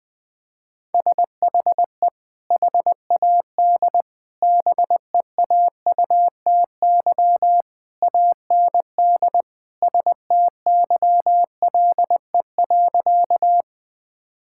Morse code, heard as a keyed tone, about 20 wpm, 700 Hz. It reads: SHE HAD BEAUTY AND STYLE.
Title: Sentences from Top 1000 Words - Review Part 5 - 20wpm